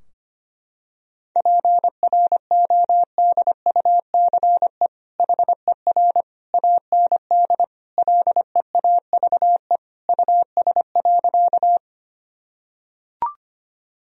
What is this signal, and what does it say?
Morse code, heard as a keyed tone, about 25 wpm, 700 Hz. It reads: PRODUCE HER AND LEAVE US.
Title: Sentences from Top 1000 Words - Review Part 3 - 25wpm